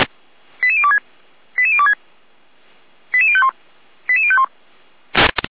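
Five-tone selective call (ZVEI), heard as quick runs of five tones, tones of about 70 ms. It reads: 88026 88026 88051 88051
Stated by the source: ZVEI five-tone selective call received by SDR.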